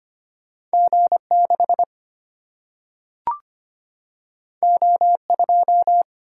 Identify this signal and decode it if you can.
Morse code, heard as a keyed tone, about 25 wpm, 700 Hz. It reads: G 6 O 2
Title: Single-Letter Single-Number 25wpm